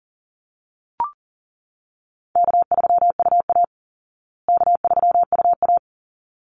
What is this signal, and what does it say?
Morse code, heard as a keyed tone, about 40 wpm, 700 Hz. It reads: K3UA K3UA
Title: Call Signs Repeated 3x - 40wpm